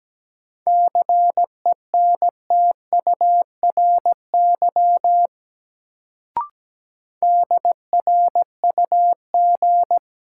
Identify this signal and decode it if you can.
Morse code, heard as a keyed tone, about 17 wpm, 700 Hz. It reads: CENTURY DRUG
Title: Sets of 5 Words - New Word Drill - 17wpm